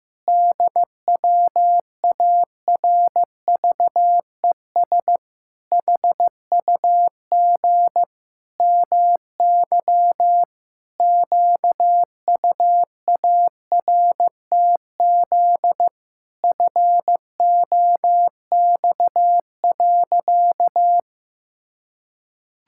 Morse code, heard as a keyed tone, about 15 wpm, 700 Hz. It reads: DWARVES HUG MY QUARTZ FOX.
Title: Pangram Challenge v2 - 15wpm